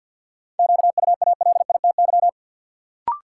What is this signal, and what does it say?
Morse code, heard as a keyed tone, about 46 wpm, 700 Hz. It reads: QUALITY
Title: Sentences from Top 1500 Words - New Word Drill Alt - 50wpm